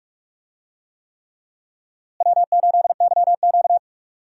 Morse code, heard as a keyed tone, about 43 wpm, 700 Hz. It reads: W8YQ
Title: USA Call Signs - 4 Characters - Repeated 3x - 45wpm